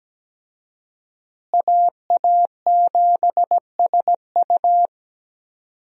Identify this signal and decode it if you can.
Morse code, heard as a keyed tone, about 17 wpm, 700 Hz. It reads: AA7SU
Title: USA Call Signs - 5 Characters - Repeated 3x - 17wpm